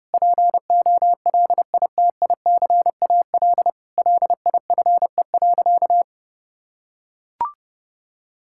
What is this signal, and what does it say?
Morse code, heard as a keyed tone, about 32 wpm, 700 Hz. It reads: POLITICAL LIFE.
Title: Sentences from Top 1000 Words - Review Part 5 - 30wpm